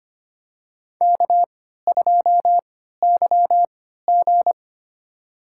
Morse code, heard as a keyed tone, about 25 wpm, spaced as for 14 wpm, 700 Hz. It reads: K2YG
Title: USA Call Signs - 4 Characters - 3x Character Spacing - 25wpm